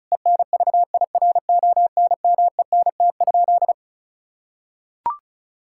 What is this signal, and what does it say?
Morse code, heard as a keyed tone, about 34 wpm, 700 Hz. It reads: ENVIRONMENT?
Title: Sentences from Top 1000 Words - Review Part 5 - 35wpm